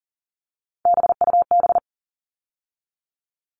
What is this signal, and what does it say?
Morse code, heard as a keyed tone, about 40 wpm, 700 Hz. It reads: BUB
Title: All Cut Numbers - 3-Digit 40wpm